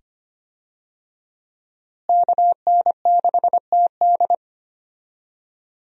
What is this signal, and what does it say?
Morse code, heard as a keyed tone, about 25 wpm, 700 Hz. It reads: KN6TD
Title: USA Call Signs - 5 Characters - Repeated 2x - 25wpm